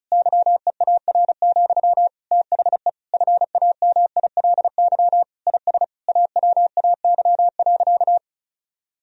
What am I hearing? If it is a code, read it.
Morse code, heard as a keyed tone, about 34 wpm, 700 Hz. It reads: YEAR, THE FAMILY IS AWAY.
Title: Sentences from Top 1000 Words - Review Part 2 - 35wpm